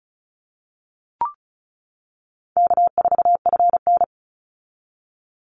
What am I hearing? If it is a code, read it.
Morse code, heard as a keyed tone, about 34 wpm, 700 Hz. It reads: K4FN
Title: Call Signs - 35wpm